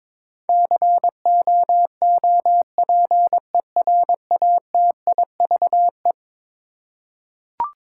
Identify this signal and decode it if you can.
Morse code, heard as a keyed tone, about 22 wpm, 700 Hz. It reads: COOPERATIVE
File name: practice / 22wpm / Eleven Letter Words - 22wpm